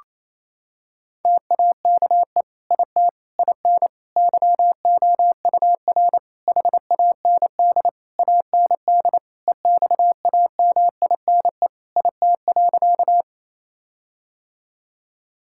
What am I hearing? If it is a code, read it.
Morse code, heard as a keyed tone, about 28 wpm, 700 Hz. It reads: TAKE IT IN YOUR HAND AND EXAMINE IT.
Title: Sentences from Top 1600 Words 28wpm